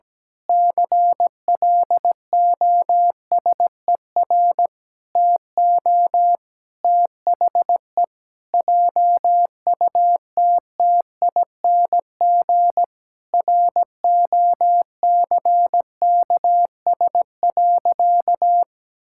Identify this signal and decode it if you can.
Morse code, heard as a keyed tone, about 17 wpm, 700 Hz. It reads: CLOSER TO THE JUTTING ROCKS.
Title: Pangram Challenge v2 - 17wpm